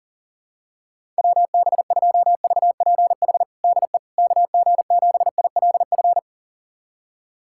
Morse code, heard as a keyed tone, about 37 wpm, 700 Hz. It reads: WB2VPH DE KG7ILF
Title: USA Callsign DE Callsign 40wpm